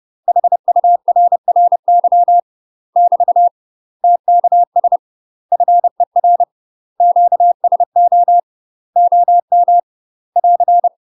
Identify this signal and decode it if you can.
Morse code, heard as a keyed tone, about 30 wpm, 700 Hz. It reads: HURRY <BT> TKS FER QSO OM <AR>
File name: UK MASTER QSOs TRAINING / 30WPM / UK MASTER QSOs 0043